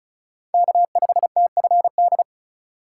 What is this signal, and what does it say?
Morse code, heard as a keyed tone, about 40 wpm, 700 Hz. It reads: K5TFD
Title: Speed Racing - USA 1x3 Call Signs 35wpm